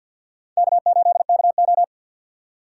Morse code, heard as a keyed tone, about 61 wpm, 700 Hz. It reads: K8KO
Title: Speed Racing - USA 1x2 Call Signs 50wpm